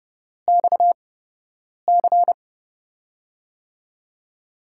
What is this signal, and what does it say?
Morse code, heard as a keyed tone, about 30 wpm, spaced as for 8 wpm, 700 Hz. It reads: XC
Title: USA Call Signs - 5 Characters - 8x Character Spacing - 30wpm